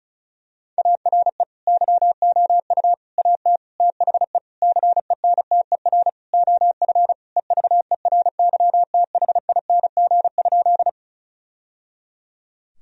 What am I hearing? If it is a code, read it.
Morse code, heard as a keyed tone, about 36 wpm, 700 Hz. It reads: ARE YOU AT THE CENTER OF EVERYTHING?
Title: Sentences from Top 1000 Words - Review Part 2 - 35wpm